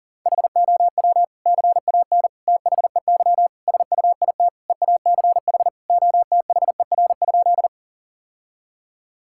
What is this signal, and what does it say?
Morse code, heard as a keyed tone, about 40 wpm, 700 Hz. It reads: HOW CAN THEY SUIT EACH OTHER?
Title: Sentences from Top 1600 Words 40wpm